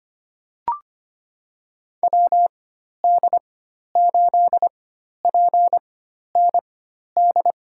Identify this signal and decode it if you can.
Morse code, heard as a keyed tone, about 25 wpm, spaced as for 12 wpm, 690 Hz. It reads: WD8PND
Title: USA Call Signs - 4x Character Spacing - 25wpm